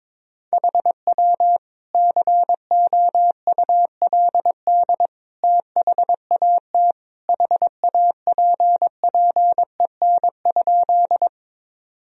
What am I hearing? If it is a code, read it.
Morse code, heard as a keyed tone, about 22 wpm, 700 Hz. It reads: HW COULD THAT HAPPEN?
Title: Sentences from Top 400 Words and Abbreviations 22wpm